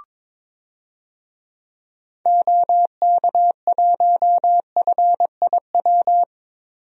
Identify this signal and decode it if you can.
Morse code, heard as a keyed tone, about 22 wpm, 700 Hz. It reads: OK1FIW
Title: DXCC-100 Call Signs - 6 Characters - 22wpm